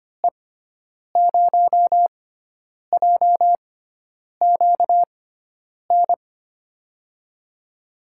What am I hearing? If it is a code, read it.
Morse code, heard as a keyed tone, about 25 wpm, spaced as for 9 wpm, 700 Hz. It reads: E0JQN
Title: USA Call Signs - 6x Character Spacing - 25wpm